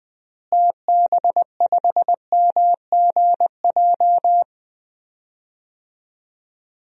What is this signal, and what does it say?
Morse code, heard as a keyed tone, about 21 wpm, 700 Hz. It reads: TB5MGJ
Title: DXCC-100 Call Signs - 6 Characters - Repeated 3x - 20wpm